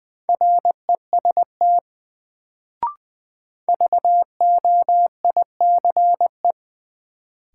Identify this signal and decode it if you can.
Morse code, heard as a keyed tone, about 20 wpm, 700 Hz. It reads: REST VOICE
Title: Sentences from Top 500 Words - New Word Drill - 20wpm